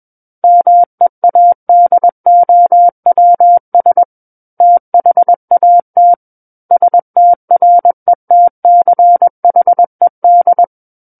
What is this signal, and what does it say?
Morse code, heard as a keyed tone, about 21 wpm, 700 Hz. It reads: MEADOWS THAT STRETCHED
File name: Book One / 12-Chapter 12 - 21 WPM